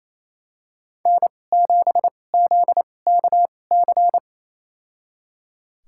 Morse code, heard as a keyed tone, about 28 wpm, spaced as for 20 wpm, 700 Hz. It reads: N7ZKC
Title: USA Call Signs - 5 Characters - 2x Character Spacing - 28wpm